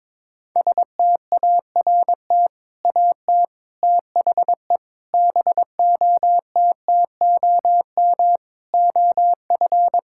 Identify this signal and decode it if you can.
Morse code, heard as a keyed tone, about 22 wpm, 700 Hz. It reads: START AT THE BOTTOM OF
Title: Sentences from Top 1000 Words - Review Part 6 - 22wpm